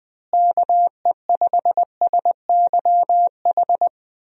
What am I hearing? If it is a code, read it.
Morse code, heard as a keyed tone, about 20 wpm, 700 Hz. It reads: KE5SYH
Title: USA Call Signs - 6 Characters - Repeated 2x - 20wpm